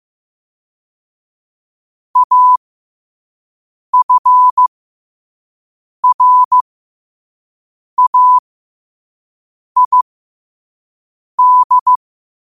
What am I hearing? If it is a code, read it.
Morse code, heard as a keyed tone, about 15 wpm, spaced as for 5 wpm, 1000 Hz. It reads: AFRAID
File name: qotdfiles / 5wpm